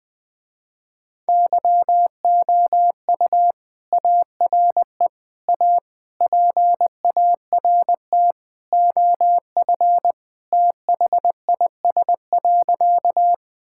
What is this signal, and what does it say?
Morse code, heard as a keyed tone, about 20 wpm, 700 Hz. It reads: YOU ARE A PART OF THIS.
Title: Sentences with 3-Word Phrases - Spoken Beginning - 20wpm